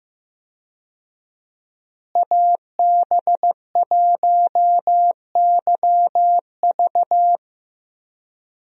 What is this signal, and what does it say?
Morse code, heard as a keyed tone, about 15 wpm, 700 Hz. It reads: AB1YV
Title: USA Call Signs - 5 Characters - Repeated 2x - 15wpm